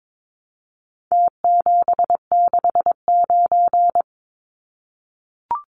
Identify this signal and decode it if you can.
Morse code, heard as a keyed tone, about 22 wpm, 700 Hz. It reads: T769
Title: Numbers - Contest Style - 4-Digit 22wpm